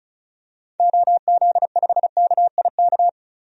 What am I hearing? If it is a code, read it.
Morse code, heard as a keyed tone, about 36 wpm, 700 Hz. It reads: OZ5KIK
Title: DXCC-100 Call Signs - 6 Characters - Repeated 2x - 35wpm